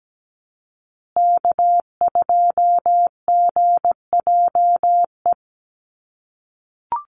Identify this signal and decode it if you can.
Morse code, heard as a keyed tone, about 17 wpm, 700 Hz. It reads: K2GJE
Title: USA Call Signs Repeated 2x 17wpm